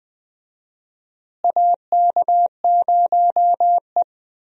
Morse code, heard as a keyed tone, about 20 wpm, 700 Hz. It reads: AK0E
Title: USA Call Signs - 4 Characters - Repeated 3x - 20wpm